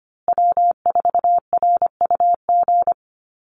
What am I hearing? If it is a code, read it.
Morse code, heard as a keyed tone, about 25 wpm, 700 Hz. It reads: W4RUG
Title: USA Call Signs Repeated 2x 25wpm